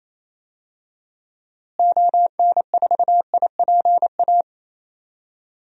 Morse code, heard as a keyed tone, about 28 wpm, 700 Hz. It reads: ON4IPA
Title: DXCC-100 Call Signs - 6 Characters - Repeated 3x - 28wpm